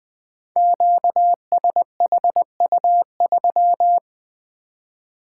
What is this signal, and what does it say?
Morse code, heard as a keyed tone, about 20 wpm, 700 Hz. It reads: QSHU3
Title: Code Groups - 5 Letter:Numbers - 20wpm